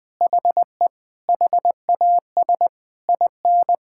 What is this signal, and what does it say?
Morse code, heard as a keyed tone, about 20 wpm, 700 Hz. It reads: HE HAS IN
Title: Sentences from Top 1000 Words - Review Part 5 - 20wpm